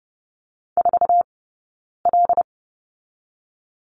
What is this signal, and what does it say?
Morse code, heard as a keyed tone, about 30 wpm, 700 Hz. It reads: 4 L
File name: Lesson 15d - Number 5 - Two Characters 30wpm